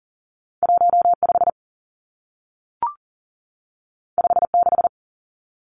Morse code, heard as a keyed tone, about 40 wpm, 700 Hz. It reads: 15 56
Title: Numbers - 2-Digit 40wpm